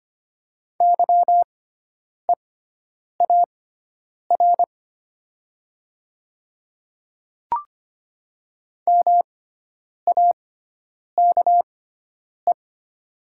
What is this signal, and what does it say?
Morse code, heard as a keyed tone, about 25 wpm, spaced as for 9 wpm, 700 Hz. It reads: YEAR MAKE
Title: 6x Character Spacing - Top 200 Words - 25wpm